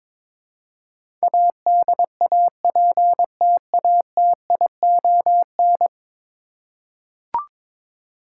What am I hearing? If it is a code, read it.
Morse code, heard as a keyed tone, about 22 wpm, 700 Hz. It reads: ADAPTATION
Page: morse-code-ninja-practice-22wpm